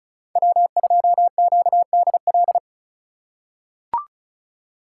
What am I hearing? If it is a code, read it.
Morse code, heard as a keyed tone, about 36 wpm, 700 Hz. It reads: W2QDL
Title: USA Call Signs - 5 Characters - 3x Character Spacing - 35wpm